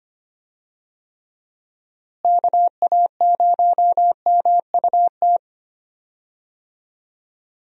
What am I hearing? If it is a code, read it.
Morse code, heard as a keyed tone, about 25 wpm, 700 Hz. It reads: KA0MUT
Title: USA Call Signs - 6 Characters - Repeated 3x - 25wpm